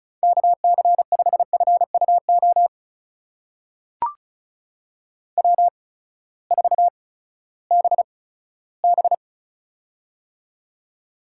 Morse code, heard as a keyed tone, about 36 wpm, spaced as for 18 wpm, 700 Hz. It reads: KC5FUO W4BB
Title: USA Call Signs - 8x Character Spacing - 35wpm